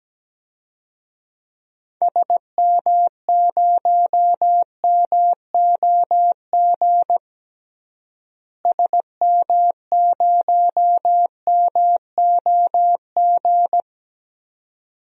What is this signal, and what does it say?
Morse code, heard as a keyed tone, about 17 wpm, 700 Hz. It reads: SM0MOG SM0MOG
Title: DXCC-100 Call Signs - 6 Characters - Repeated 2x - 17wpm